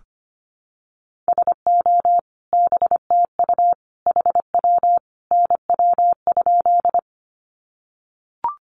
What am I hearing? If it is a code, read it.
Morse code, heard as a keyed tone, about 25 wpm, 700 Hz. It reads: SO BTU HW NW?
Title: Common QSO Phrases 25wpm